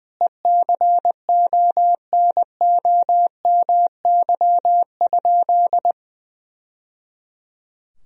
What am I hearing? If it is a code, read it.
Morse code, heard as a keyed tone, about 20 wpm, 700 Hz. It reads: ECONOMY?
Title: Sentences from Top 1000 Words - Review Part 3 - 20wpm